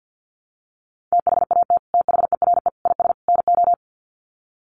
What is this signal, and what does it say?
Morse code, heard as a keyed tone, about 48 wpm, 700 Hz. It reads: THAT THERE IS NO
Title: Sets of 4 Words 50wpm